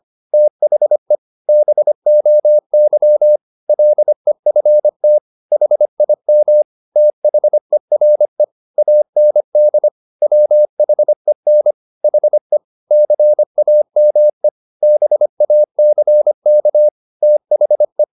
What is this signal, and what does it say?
Morse code, heard as a keyed tone, about 25 wpm, 600 Hz. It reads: THE BOY LEFT HIM THERE AND WHEN HE CAME BACK THE
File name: the old man and the sea part 1 25wpm